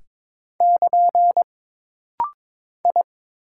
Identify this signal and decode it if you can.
Morse code, heard as a keyed tone, about 22 wpm, 700 Hz. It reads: <KN> I
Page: This rapid-fire timing is geared to push your ICR (Instant Character Recognition) skills.